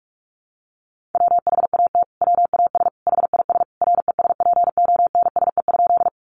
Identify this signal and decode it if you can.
Morse code, heard as a keyed tone, about 44 wpm, 700 Hz. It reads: WHAT WAS HIS RESPONSE?
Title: Sentences from Top 1400 Words 45wpm